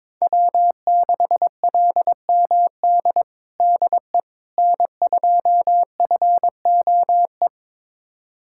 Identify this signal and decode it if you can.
Morse code, heard as a keyed tone, about 22 wpm, 700 Hz. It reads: W6LMD DE N2FOE
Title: USA Callsign DE Callsign - 5 Characters - 22wpm